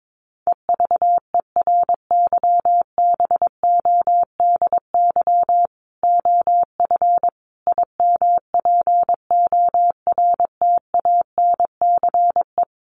Morse code, heard as a keyed tone, about 22 wpm, 700 Hz. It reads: EVERYBODY OF IMPORTANCE
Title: Sentences from Top 1400 Words 22wpm